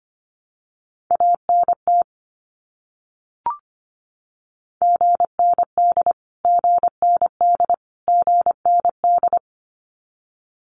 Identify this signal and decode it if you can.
Morse code, heard as a keyed tone, about 25 wpm, 700 Hz. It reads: ANT GND GND GND
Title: Abbreviations Repeated 3x - 25wpm